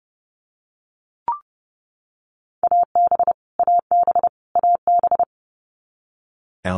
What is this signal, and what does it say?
Morse code, heard as a keyed tone, about 30 wpm, 700 Hz. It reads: AB AB AB
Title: Canadian Provinces and Territories - Repeated 3x - 30wpm